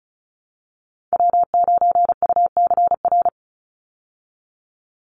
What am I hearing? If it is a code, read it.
Morse code, heard as a keyed tone, about 34 wpm, 700 Hz. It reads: W9UCR
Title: Call Signs Repeated 3x - 35wpm